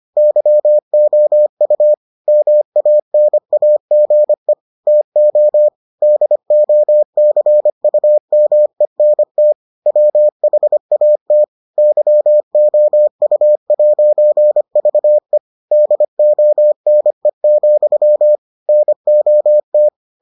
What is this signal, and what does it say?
Morse code, heard as a keyed tone, about 25 wpm, 600 Hz. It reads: YOU MANAGE TO DOCUMENT WHAT YOU'VE DONE, NOT